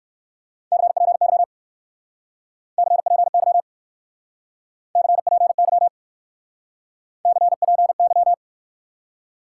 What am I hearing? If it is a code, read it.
Morse code, heard as a keyed tone, about 61 wpm, 700 Hz. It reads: CPY CPY CPY CPY